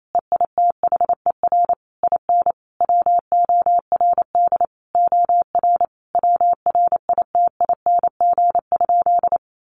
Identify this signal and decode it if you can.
Morse code, heard as a keyed tone, about 28 wpm, 700 Hz. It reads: EITHER IN WORD OR WRITING?
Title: Sentences from Top 1200 Words 28wpm